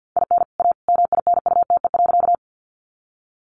Morse code, heard as a keyed tone, about 50 wpm, 700 Hz. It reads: IN A MINUTE.
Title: Sentences from Top 500 Words 50wpm